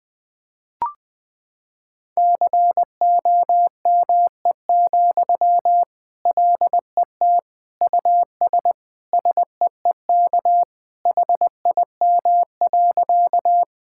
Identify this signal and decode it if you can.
Morse code, heard as a keyed tone, about 20 wpm, 700 Hz. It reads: COME, LET US SEEK HIM.
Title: Sentences from Top 1600 Words 20wpm